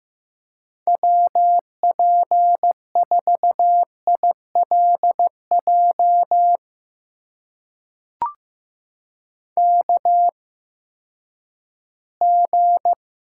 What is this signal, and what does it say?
Morse code, heard as a keyed tone, about 15 wpm, spaced as for 10 wpm, 700 Hz. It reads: WP4ILJ KG
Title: USA Call Signs - 6 Characters - 8x Character Spacing - 15wpm